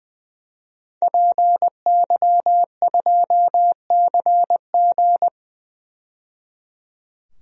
Morse code, heard as a keyed tone, about 20 wpm, 700 Hz. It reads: PY2CG
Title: DXCC-100 Call Signs - 5 Characters - 20wpm